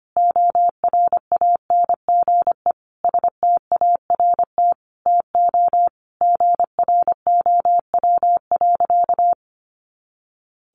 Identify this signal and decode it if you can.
Morse code, heard as a keyed tone, about 25 wpm, 700 Hz. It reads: ORANGE START TO GROW.
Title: Easy Phrases 25wpm